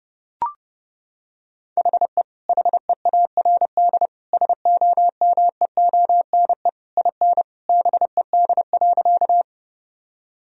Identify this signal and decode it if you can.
Morse code, heard as a keyed tone, about 32 wpm, 700 Hz. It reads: HE HEARD SOMEONE IN BED.
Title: Sentences from Top 1000 Words - Review Part 4 - 30wpm